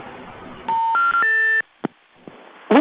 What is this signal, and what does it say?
Special information tone.